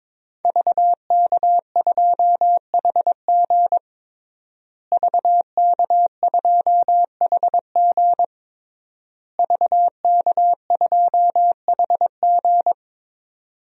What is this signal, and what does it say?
Morse code, heard as a keyed tone, about 22 wpm, 700 Hz. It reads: VK2HG VK2HG VK2HG
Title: DXCC-100 Call Signs - 5 Characters - Repeated 3x - 22wpm